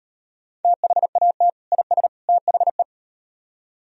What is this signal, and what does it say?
Morse code, heard as a keyed tone, about 36 wpm, 700 Hz. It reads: THAT IS THE